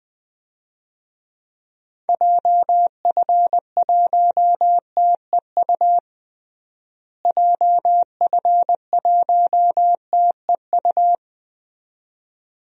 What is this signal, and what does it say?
Morse code, heard as a keyed tone, about 21 wpm, 700 Hz. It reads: JF1TEU JF1TEU
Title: DXCC-100 Call Signs - 6 Characters - Repeated 2x - 20wpm